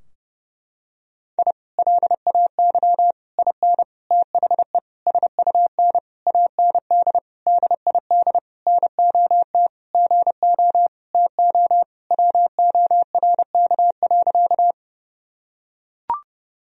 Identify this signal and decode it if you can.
Morse code, heard as a keyed tone, about 30 wpm, 700 Hz. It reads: I LAY IN THE SUN AND DID NOT GO TO WORK.
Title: Sentences from Top 1000 Words - Review Part 2 - 30wpm